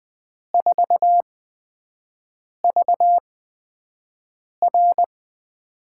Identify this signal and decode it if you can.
Morse code, heard as a keyed tone, about 20 wpm, spaced as for 5 wpm, 700 Hz. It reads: 4VR
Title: USA Call Signs - 8x Character Spacing - 20wpm